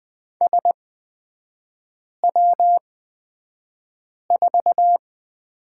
Morse code, heard as a keyed tone, about 20 wpm, spaced as for 5 wpm, 700 Hz. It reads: SW4
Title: 255 Lesson 41e - Period - Three Characters 20wpm